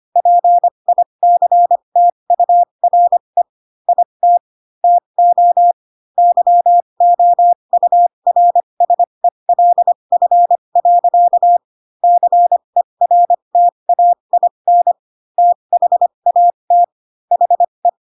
Morse code, heard as a keyed tone, about 25 wpm, spaced as for 22 wpm, 700 Hz. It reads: PICTURE IT TO YOURSELF. CERTAIN THAT HE